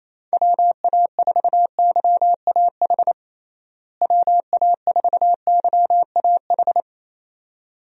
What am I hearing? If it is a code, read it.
Morse code, heard as a keyed tone, about 28 wpm, 700 Hz. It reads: WA4YAH WA4YAH
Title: USA Call Signs - 6 Characters - Repeated 2x - 28wpm